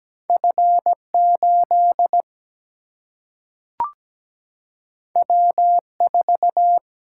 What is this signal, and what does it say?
Morse code, heard as a keyed tone, about 17 wpm, 700 Hz. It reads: F8 W4
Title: Single-Letter Single-Number 17wpm